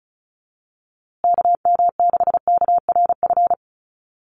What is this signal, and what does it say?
Morse code, heard as a keyed tone, about 34 wpm, 700 Hz. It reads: KM6KRF
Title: USA Call Signs Repeated 3x 35wpm